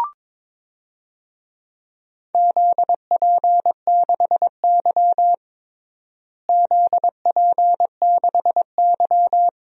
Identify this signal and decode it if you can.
Morse code, heard as a keyed tone, about 22 wpm, 700 Hz. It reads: ZP6Y ZP6Y
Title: DXCC-100 Call Signs - 4 Characters - Repeated 2x - 22wpm